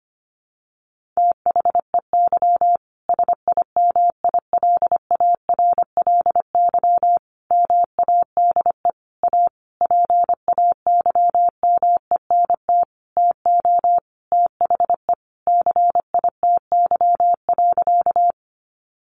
Morse code, heard as a keyed tone, about 25 wpm, 700 Hz. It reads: THEY SIMILARLY MADE A PAYMENT TO THE CITY.
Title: Sentences from Top 1400 Words 25wpm